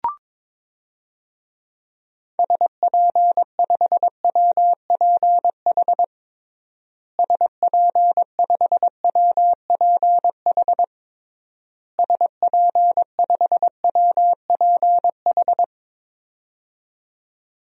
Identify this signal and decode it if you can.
Morse code, heard as a keyed tone, about 22 wpm, 700 Hz. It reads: SP5WPH SP5WPH SP5WPH
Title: DXCC-100 Call Signs - 6 Characters - Repeated 3x - 22wpm